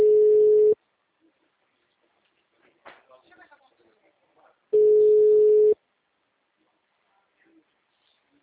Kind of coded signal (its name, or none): telephone call progress tone